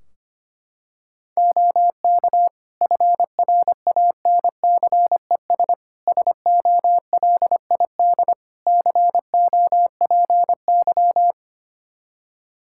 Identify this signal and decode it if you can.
Morse code, heard as a keyed tone, about 25 wpm, 700 Hz. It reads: OK FRANCES SOLID COPY
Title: Practice QSO - Exchange 3 - More Info - 25wpm